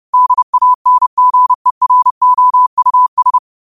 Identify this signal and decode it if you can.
Morse code, heard as a keyed tone, about 30 wpm, 1000 Hz. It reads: DANGEROUS
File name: qotdfiles / 30wpm